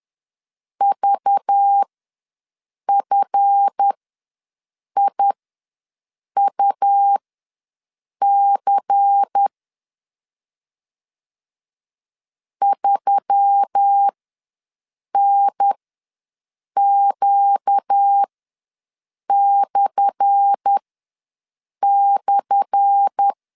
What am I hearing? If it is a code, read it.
Morse code, heard as a keyed tone, about 10 wpm, spaced as for 6 wpm, 800 Hz. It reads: VFIUC 3NQ//
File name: LEZIONI / 10 SIGNS / 10A SLASH ws 10wpm 4121